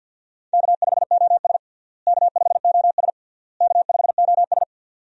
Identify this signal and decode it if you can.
Morse code, heard as a keyed tone, about 52 wpm, 700 Hz. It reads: K5OS K5OS K5OS
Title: USA Call Signs - 4 Characters - Repeated 3x - 50wpm